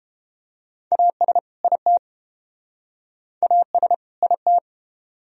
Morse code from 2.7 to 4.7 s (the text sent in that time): AS IT